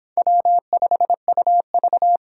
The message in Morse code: W5UV

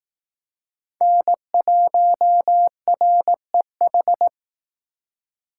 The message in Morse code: N1REH

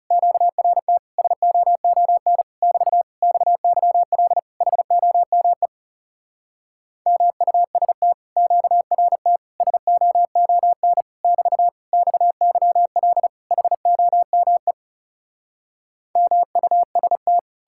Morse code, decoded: QRT SOON <BT> XYL HOME MUST QRT SOON <BT> XYL HOME MUST

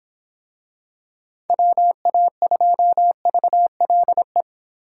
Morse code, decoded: WA2VLE

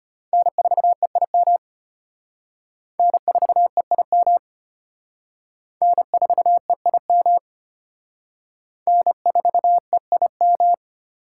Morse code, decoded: N4EIM N4EIM N4EIM N4EIM

